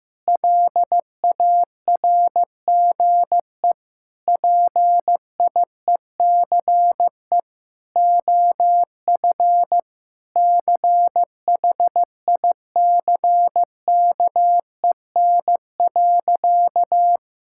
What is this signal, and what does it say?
Morse code, heard as a keyed tone, about 15 wpm, 700 Hz. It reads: LARGE PIECE OF CHICKEN.
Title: Sentences from Top 1000 Words - Review Part 5 - 15wpm